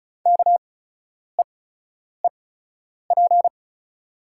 Morse code, heard as a keyed tone, about 36 wpm, spaced as for 10 wpm, 700 Hz. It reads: KEEP
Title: 8x Character Spacing - Top 300 Words - 35wpm